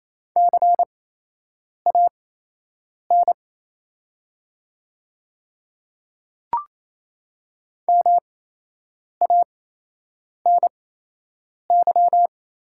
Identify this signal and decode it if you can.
Morse code, heard as a keyed tone, about 27 wpm, spaced as for 8 wpm, 700 Hz. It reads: CAN MANY
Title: 8x Character Spacing - Top 100 Words - 28wpm